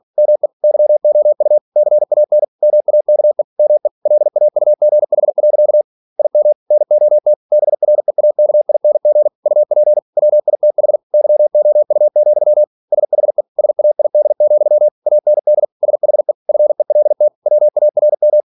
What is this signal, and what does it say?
Morse code, heard as a keyed tone, about 46 wpm, 600 Hz. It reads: ME YOU CAN MAKE ME LAUGH. IM NOT BREAKING UP WITH YOU, SHE SAID, AND SHE FELT WARM
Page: Never 47wpm